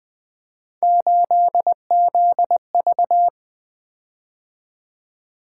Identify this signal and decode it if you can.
Morse code, heard as a keyed tone, about 20 wpm, 700 Hz. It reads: 8ZV
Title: Three Letter-Number Combinations - 20wpm